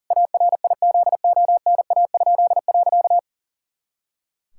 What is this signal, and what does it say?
Morse code, heard as a keyed tone, about 40 wpm, 700 Hz. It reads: ARIZONA?.
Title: Pangram Challenge v2 - 40wpm